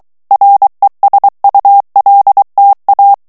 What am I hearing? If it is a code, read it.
Morse code, heard as a keyed tone, about 23 wpm, 800 Hz. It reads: RESULTA
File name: morse mp3 / ESPAGNOL TEXTE EN CLAIR 2 VITESSE 22 MOTS MINUTES